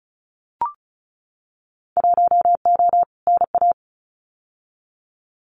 Morse code, heard as a keyed tone, about 36 wpm, 700 Hz. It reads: 1O NA
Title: Winter Field Day Exchanges 35wpm